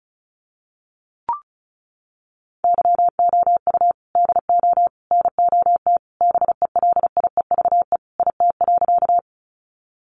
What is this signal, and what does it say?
Morse code, heard as a keyed tone, about 34 wpm, 700 Hz. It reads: YOU DO NOT BELIEVE IT.
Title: Sentences from Top 1000 Words 35wpm